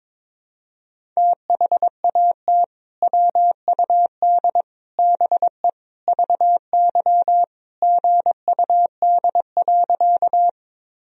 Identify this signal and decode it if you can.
Morse code, heard as a keyed tone, about 22 wpm, 700 Hz. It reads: THAT WUD BE VY GUD.
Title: Sentences from Top 200 Words and Abbreviations 22wpm